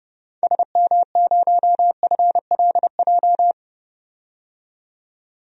Morse code, heard as a keyed tone, about 31 wpm, 700 Hz. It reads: SM0FLJ
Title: DXCC-100 Call Signs - 6 Characters - 30wpm